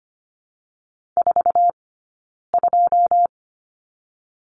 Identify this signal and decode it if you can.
Morse code, heard as a keyed tone, about 25 wpm, spaced as for 9 wpm, 700 Hz. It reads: 42
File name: Lesson 15d - Number 5 - Two Characters 25wpm